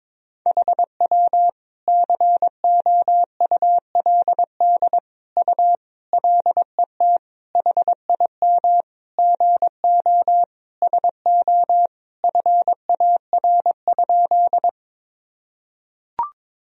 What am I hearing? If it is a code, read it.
Morse code, heard as a keyed tone, about 22 wpm, 700 Hz. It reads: HW COULD U LET HIM GO SO FAR?
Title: Sentences from Top 300 Words and Abbreviations 22wpm